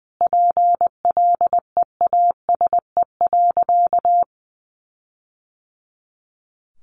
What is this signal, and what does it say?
Morse code, heard as a keyed tone, about 20 wpm, 700 Hz. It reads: PLEASE.
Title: Sentences from Top 1500 Words 20wpm